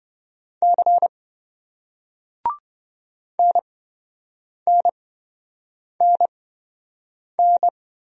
Morse code, heard as a keyed tone, about 27 wpm, spaced as for 7 wpm, 700 Hz. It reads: C NNNN